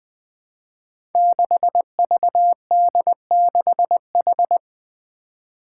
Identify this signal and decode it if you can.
Morse code, heard as a keyed tone, about 20 wpm, 700 Hz. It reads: 6VD6H
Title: Five Challenging Letter-Number Combinations 20wpm